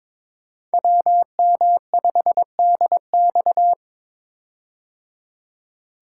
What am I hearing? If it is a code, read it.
Morse code, heard as a keyed tone, about 22 wpm, 700 Hz. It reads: WM5DX
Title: K1USN SST QSO Practice - 22wpm